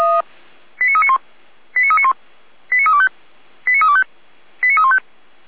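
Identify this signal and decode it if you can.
Five-tone selective call (ZVEI), heard as quick runs of five tones, tones of about 70 ms. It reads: 89391 89391 89325 89325 89315
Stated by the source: Five-tone ZVEI selective call received by SDR.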